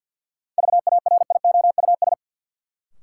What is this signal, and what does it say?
Morse code, heard as a keyed tone, about 52 wpm, 700 Hz. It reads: VARIOUS
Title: Sentences from Top 1000 Words - Review Part 4 - New Word Drill - 50wpm